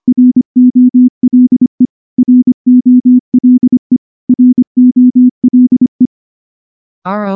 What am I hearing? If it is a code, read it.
Morse code, heard as a keyed tone, about 24 wpm, 250 Hz. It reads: ROLE ROLE ROLE
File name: morse / morse-training-25WPM